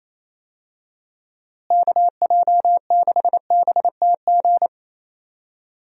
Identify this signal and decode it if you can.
Morse code, heard as a keyed tone, about 28 wpm, 700 Hz. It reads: KJ6BTG
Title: USA Call Signs - 6 Characters - Repeated 3x - 28wpm